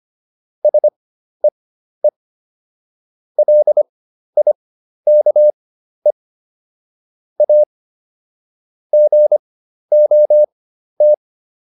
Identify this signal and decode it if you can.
Morse code, heard as a keyed tone, about 25 wpm, spaced as for 12 wpm, 600 Hz. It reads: SEE LIKE A GOT